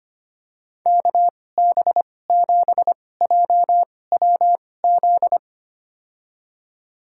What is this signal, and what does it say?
Morse code, heard as a keyed tone, about 25 wpm, spaced as for 18 wpm, 700 Hz. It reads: KB7JWZ
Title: USA Call Signs - 2x Character Spacing - 25wpm